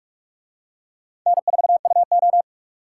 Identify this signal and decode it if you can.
Morse code, heard as a keyed tone, about 44 wpm, 700 Hz. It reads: N4UO